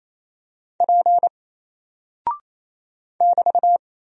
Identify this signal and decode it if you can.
Morse code, heard as a keyed tone, about 27 wpm, 700 Hz. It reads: P <BT>
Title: Single Character (Letter:Number:Punctuation:Prosign) Alt 28wpm